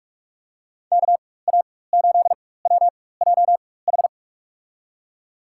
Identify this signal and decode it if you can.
Morse code, heard as a keyed tone, about 44 wpm, spaced as for 21 wpm, 690 Hz. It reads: KA8WJH
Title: USA Call Signs - 4x Character Spacing - 45wpm